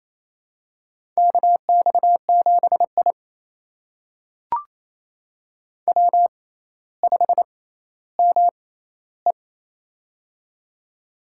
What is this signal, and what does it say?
Morse code, heard as a keyed tone, about 27 wpm, spaced as for 14 wpm, 700 Hz. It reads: KX7I W5ME
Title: USA Call Signs - 4 Characters - 6x Character Spacing - 28wpm